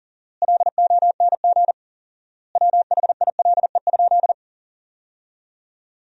Morse code, heard as a keyed tone, about 40 wpm, 700 Hz. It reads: LONG WHILE?
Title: 4x Word-Spacing - Sentences from Top 300 Words - 40wpm